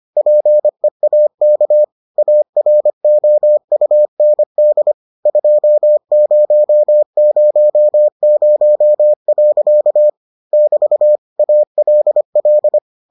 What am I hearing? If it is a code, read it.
Morse code, heard as a keyed tone, about 25 wpm, 600 Hz. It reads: PEAK AROUND 2000. <BT> ALL